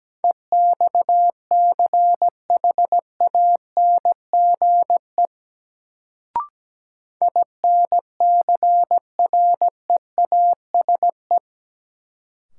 Morse code, heard as a keyed tone, about 17 wpm, 700 Hz. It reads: EXCHANGE INCREASE
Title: Sets of 5 Words - New Word Drill - 17wpm